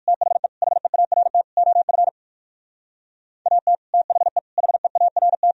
Morse code, heard as a keyed tone, about 58 wpm, 700 Hz. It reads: THE HEART OF AT THE HEART